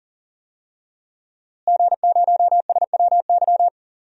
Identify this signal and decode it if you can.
Morse code, heard as a keyed tone, about 40 wpm, 700 Hz. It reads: G0SWY
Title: DXCC-100 Call Signs - 5 Characters - Repeated 2x - 40wpm